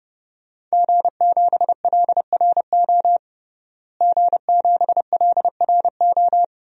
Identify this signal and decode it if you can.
Morse code, heard as a keyed tone, about 30 wpm, 700 Hz. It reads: G7LRO G7LRO
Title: DXCC-100 Call Signs - 5 Characters - Repeated 3x - 30wpm